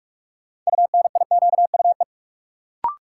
Morse code, heard as a keyed tone, about 46 wpm, 700 Hz. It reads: UNIQUE
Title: Sentences from Top 1000 Words - Review Part 7 - New Word Drill - 45wpm